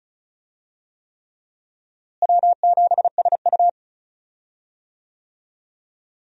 Morse code, heard as a keyed tone, about 34 wpm, 700 Hz. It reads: W7SU
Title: USA Call Signs - 4 Characters - Repeated 3x - 35wpm